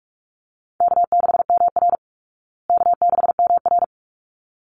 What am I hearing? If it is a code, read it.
Morse code, heard as a keyed tone, about 44 wpm, 700 Hz. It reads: K6MR K6MR
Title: Easy Call Signs Repeated 2x - 45wpm